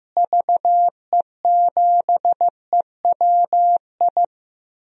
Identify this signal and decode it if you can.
Morse code, heard as a keyed tone, about 15 wpm, 700 Hz. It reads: VE7EWI
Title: DXCC-100 Call Signs - 6 Characters - Repeated 2x - 15wpm